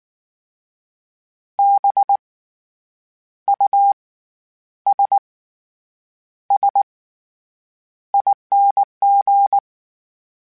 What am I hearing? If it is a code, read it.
Morse code, heard as a keyed tone, about 19 wpm, spaced as for 8 wpm, 800 Hz. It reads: BUSSING
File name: Special INGs / Special INGs-mp3s / SlowToFaster ing @18